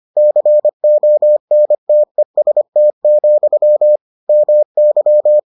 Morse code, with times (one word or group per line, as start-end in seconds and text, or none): 0.2-4.0 CONTEST,
4.3-5.4 MY